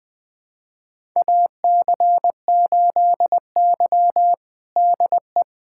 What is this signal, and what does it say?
Morse code, heard as a keyed tone, about 20 wpm, 700 Hz. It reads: AC8Y DE
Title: USA Callsign DE Callsign - 4 Characters - 20wpm